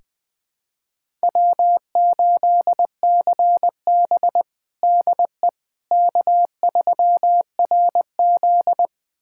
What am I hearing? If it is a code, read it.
Morse code, heard as a keyed tone, about 20 wpm, 700 Hz. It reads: W8CB DE K3RZ
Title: USA Callsign DE Callsign - 4 Characters - 20wpm